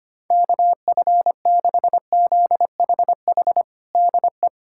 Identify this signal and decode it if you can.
Morse code, heard as a keyed tone, about 25 wpm, 700 Hz. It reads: KF6ZHH DE
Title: USA Callsign DE Callsign 25wpm